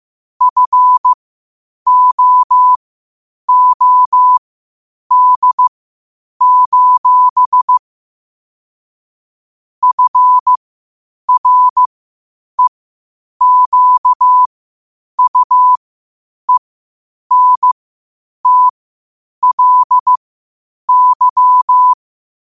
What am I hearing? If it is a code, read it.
Morse code, heard as a keyed tone, about 15 wpm, spaced as for 8 wpm, 1000 Hz. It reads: FOOD: FREQUENTLY
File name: qotdfiles / 8wpm